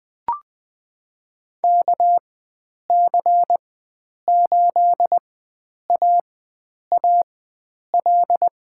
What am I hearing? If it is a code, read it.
Morse code, heard as a keyed tone, about 20 wpm, spaced as for 9 wpm, 700 Hz. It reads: KC8AAL
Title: USA Call Signs - 4x Character Spacing - 20wpm